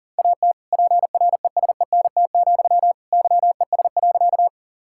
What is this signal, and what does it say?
Morse code, heard as a keyed tone, about 37 wpm, 700 Hz. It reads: AT PRESENT, YES.
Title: Sentences from Top 1000 Words - Review Part 5 - 40wpm